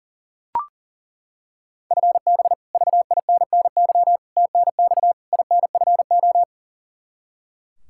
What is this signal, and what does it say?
Morse code, heard as a keyed tone, about 40 wpm, 700 Hz. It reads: FB VINNY TNX INFO
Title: Practice QSO - Exchange 3 - More Info - 40wpm